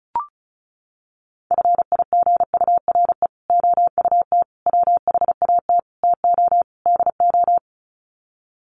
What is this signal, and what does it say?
Morse code, heard as a keyed tone, about 34 wpm, 700 Hz. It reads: FIGURE OUT WHAT TO DO
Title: Sets of 5 Words Encore2 35wpm